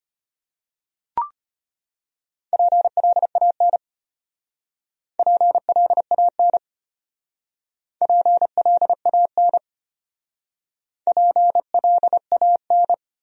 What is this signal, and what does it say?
Morse code, heard as a keyed tone, about 31 wpm, 700 Hz. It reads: PLAN PLAN PLAN PLAN
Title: Speed-Racing Top 500 Words 25wpm